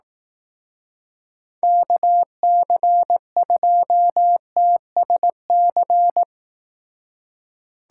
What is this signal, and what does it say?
Morse code, heard as a keyed tone, about 18 wpm, 700 Hz. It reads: KC2TSC
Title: Speed Racing - USA Call Signs 15wpm